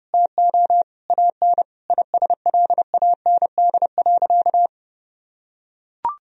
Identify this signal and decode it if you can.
Morse code, heard as a keyed tone, about 30 wpm, 700 Hz. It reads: TO AN ISLAND.
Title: Sentences from Top 1000 Words - Review Part 2 - 30wpm